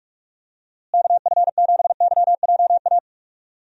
Morse code, heard as a keyed tone, about 48 wpm, 700 Hz. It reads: KF7YJA
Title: USA Call Signs - 6 Characters - Repeated 3x - 45wpm